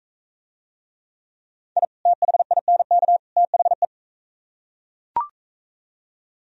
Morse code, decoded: I THINK THE

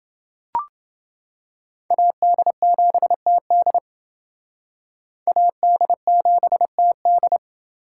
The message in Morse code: AD7TD AD7TD

